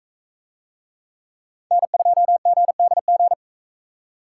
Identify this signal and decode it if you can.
Morse code, heard as a keyed tone, about 41 wpm, 700 Hz. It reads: N2GDG